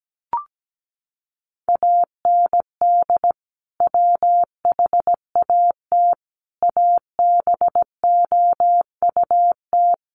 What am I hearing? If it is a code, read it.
Morse code, heard as a keyed tone, about 17 wpm, 700 Hz. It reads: AND WHAT ABOUT
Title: Sentences from Top 200 Words 17wpm